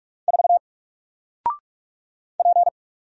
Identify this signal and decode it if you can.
Morse code, heard as a keyed tone, about 44 wpm, 700 Hz. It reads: 4 P
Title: Single Character (Letter:Number:Punctuation:Prosign) Alt 45wpm